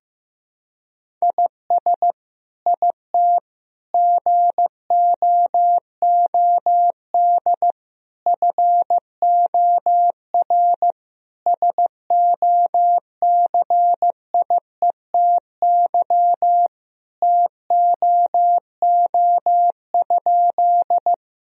Morse code, heard as a keyed tone, about 15 wpm, 700 Hz. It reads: IS IT GOOD FOR SOCIETY TOO?
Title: Sentences from Top 1000 Words - Review Part 4 - 15wpm